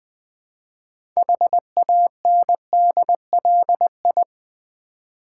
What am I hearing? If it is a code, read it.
Morse code, heard as a keyed tone, about 20 wpm, 700 Hz. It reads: HANDLI
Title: Build A Word - 2 Letters At A Time - 8-Letter Words - 20wpm